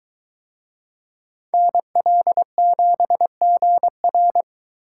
Morse code, heard as a keyed tone, about 21 wpm, 700 Hz. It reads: NL7GR